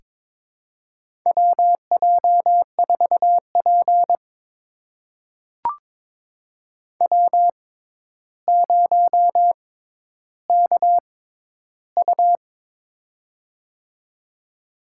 Morse code, heard as a keyed tone, about 22 wpm, spaced as for 11 wpm, 700 Hz. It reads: WJ4P W0KU
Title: USA Call Signs - 6x Character Spacing - 22wpm